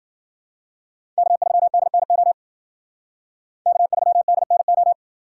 Morse code, heard as a keyed tone, about 51 wpm, 700 Hz. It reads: K3DNO K3DNO